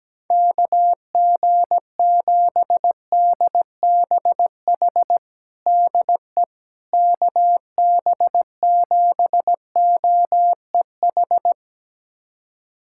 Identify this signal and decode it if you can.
Morse code, heard as a keyed tone, about 17 wpm, 700 Hz. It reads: KG7DBH DE KB7OEH
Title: USA Callsign DE Callsign - 6 Characters - 17wpm